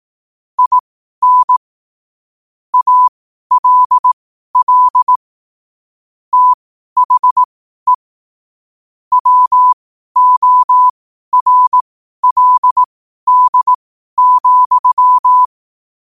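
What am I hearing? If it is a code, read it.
Morse code, heard as a keyed tone, about 18 wpm, spaced as for 12 wpm, 1000 Hz. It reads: IN ALL THE WORLD,
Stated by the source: Morse question-of-the-day sent at 12 wpm.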